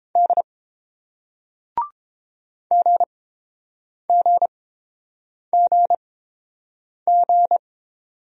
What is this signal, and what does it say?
Morse code, heard as a keyed tone, about 30 wpm, spaced as for 7 wpm, 700 Hz. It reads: D GGGG